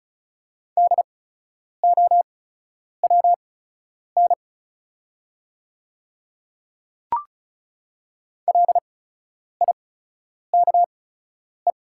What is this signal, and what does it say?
Morse code, heard as a keyed tone, about 36 wpm, spaced as for 10 wpm, 700 Hz. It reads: DOWN LIKE
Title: 8x Character Spacing - Top 200 Words - 35wpm